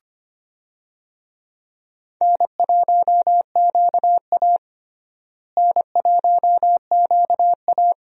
Morse code, heard as a keyed tone, about 25 wpm, 700 Hz. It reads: N1QA N1QA
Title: USA Call Signs - 4 Characters - Repeated 2x - 25wpm